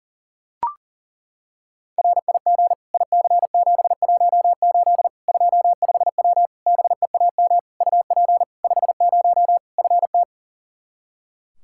Morse code, heard as a keyed tone, about 40 wpm, 700 Hz. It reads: RIG IC718 25W BEAM UP 50 FT
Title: K1USN SST QSO Practice 40wpm